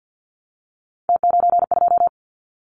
Morse code, heard as a keyed tone, about 50 wpm, 700 Hz. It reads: T92
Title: Callsign Prefixes for Top 50 DXCC - 50wpm